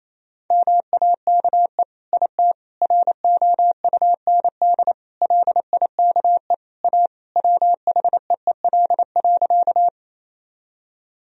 Morse code, heard as a keyed tone, about 28 wpm, 700 Hz. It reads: MAKE IT ROUND LIKE A WHEEL.
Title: Sentences from Top 1000 Words - Review Part 3 - 28wpm